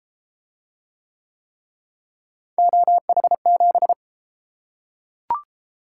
Morse code, heard as a keyed tone, about 31 wpm, 700 Hz. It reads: OH7